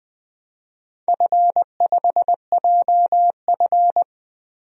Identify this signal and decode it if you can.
Morse code, heard as a keyed tone, about 20 wpm, 700 Hz. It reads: F5JF
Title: DXCC-100 Call Signs - 4 Characters - 20wpm